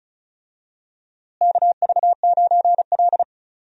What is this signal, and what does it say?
Morse code, heard as a keyed tone, about 36 wpm, 700 Hz. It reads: KV9L
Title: USA Call Signs - 4 Characters - Repeated 2x - 35wpm